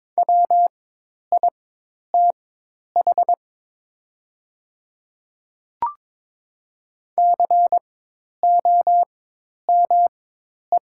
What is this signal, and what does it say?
Morse code, heard as a keyed tone, about 22 wpm, spaced as for 10 wpm, 700 Hz. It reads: WITH COME
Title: 4x Character Spacing - Top 100 Words - 22wpm